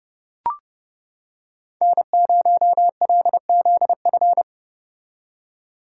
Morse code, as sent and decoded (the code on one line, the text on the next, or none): -. ----- .-.. --.. ..-.
N0LZF